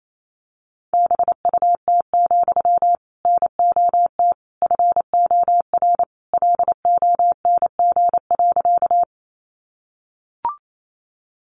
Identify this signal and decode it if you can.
Morse code, heard as a keyed tone, about 28 wpm, 700 Hz. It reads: BUT, NOT FOR LONG.
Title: Sentences from Top 600 Words 28wpm